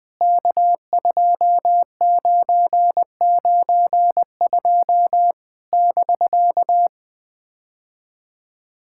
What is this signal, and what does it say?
Morse code, heard as a keyed tone, about 20 wpm, 700 Hz. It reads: K2992 <BK>
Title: POTA QSO Practice - 20wpm